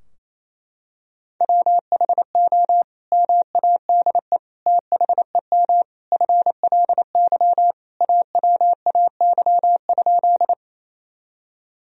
Morse code, heard as a keyed tone, about 28 wpm, 700 Hz. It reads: WHO MADE THEM FLY AWAY?
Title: Sentences from Top 1000 Words - Review Part 2 - 28wpm